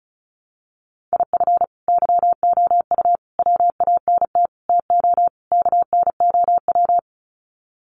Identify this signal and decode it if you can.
Morse code, heard as a keyed tone, about 34 wpm, 700 Hz. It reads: IF YOU WANT TO KNOW
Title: Sets of 5 Words Encore2 35wpm